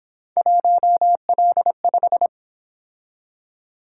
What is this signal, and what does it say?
Morse code, heard as a keyed tone, about 27 wpm, 700 Hz. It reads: 1L5
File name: Speed-Racing - Three Letter-Number Combinations - 22wpm